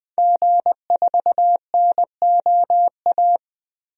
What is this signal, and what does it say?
Morse code, heard as a keyed tone, about 20 wpm, 700 Hz. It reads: G4NOA
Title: DXCC-100 Call Signs - 5 Characters - Repeated 3x - 20wpm